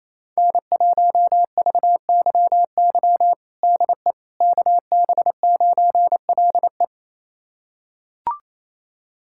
Morse code, heard as a keyed tone, about 27 wpm, 700 Hz. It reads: N1VYY DE KB9LE
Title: USA Callsign DE Callsign - 5 Characters - 28wpm